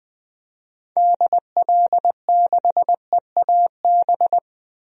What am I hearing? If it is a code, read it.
Morse code, heard as a keyed tone, about 21 wpm, 700 Hz. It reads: DL6EAB
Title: DXCC-100 Call Signs - 6 Characters - 20wpm